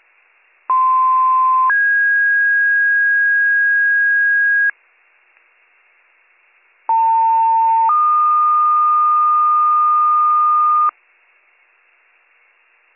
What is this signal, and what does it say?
Two-tone page, heard as a one-second tone, then a three-second tone, about 1040 then 1650 Hz, about 900 then 1220 Hz. It is heard from 0.7 to 10.9 s.